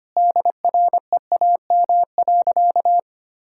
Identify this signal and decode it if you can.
Morse code, heard as a keyed tone, about 25 wpm, 700 Hz. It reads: DREAM.
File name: practice / 25wpm / Sentences with 3-Word Phrases - 25wpm